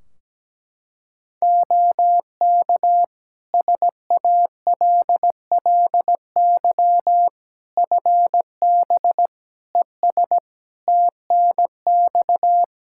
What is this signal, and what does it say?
Morse code, heard as a keyed tone, about 17 wpm, 700 Hz. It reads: OK SALLY FB ES TNX
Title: Practice QSO - Exchange 2 - Info - 17wpm